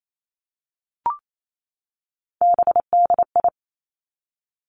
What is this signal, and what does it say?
Morse code, heard as a keyed tone, about 28 wpm, 700 Hz. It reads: BDI
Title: Three Challenging Letter-Number Combinations 28wpm